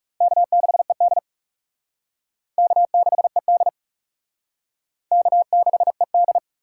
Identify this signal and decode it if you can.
Morse code, heard as a keyed tone, about 38 wpm, 700 Hz. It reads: K6ED K6ED K6ED